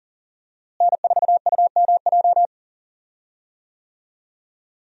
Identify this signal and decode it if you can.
Morse code, heard as a keyed tone, about 40 wpm, 700 Hz. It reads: N4UMJ
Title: USA Call Signs - 5 Characters - Repeated 2x - 40wpm